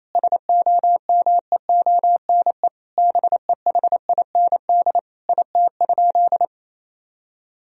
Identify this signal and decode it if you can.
Morse code, heard as a keyed tone, about 28 wpm, 700 Hz. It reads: SOMEONE BEHIND IT?
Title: Sentences from Top 1000 Words - Review Part 4 - 28wpm